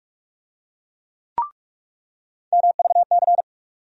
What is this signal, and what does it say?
Morse code, heard as a keyed tone, about 42 wpm, 700 Hz. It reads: MVC
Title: USA Call Sign Postfixes - 3 Letters 45wpm